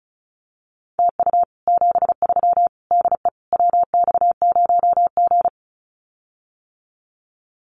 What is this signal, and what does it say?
Morse code, heard as a keyed tone, about 35 wpm, 700 Hz. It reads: TU 73 DE WX0G
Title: Common QSO Phrases Encore 35wpm